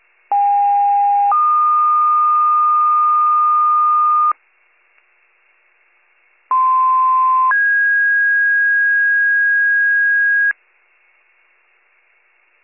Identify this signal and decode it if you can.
Two-tone page, heard as a one-second tone, then a three-second tone, about 810 then 1220 Hz, about 1040 then 1650 Hz. It is heard from 0.3 to 10.5 s.